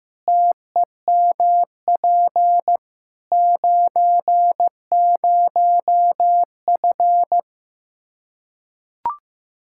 Morse code, heard as a keyed tone, about 16 wpm, 700 Hz. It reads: TEMP 90F
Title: Practice QSO - Exchange 2 - Info - 15wpm